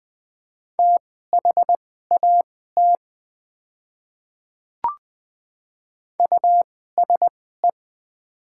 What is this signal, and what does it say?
Morse code, heard as a keyed tone, about 21 wpm, spaced as for 15 wpm, 700 Hz. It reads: THAT USE
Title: 2x Character Spacing - Top 100 Words - 20wpm